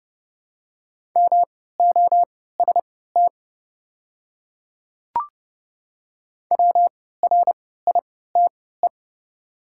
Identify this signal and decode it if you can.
Morse code, heard as a keyed tone, about 29 wpm, spaced as for 17 wpm, 690 Hz. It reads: MOST WRITE
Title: 3x Character Spacing - Top 100 Words - 30wpm